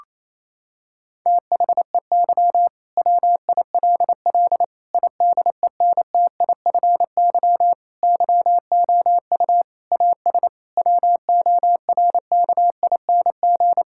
Morse code, heard as a keyed tone, about 28 wpm, 700 Hz. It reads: THEY WILL IDENTIFY YOU AS WORKING
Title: Sentences from Top 1000 Words - Review Part 6 - 28wpm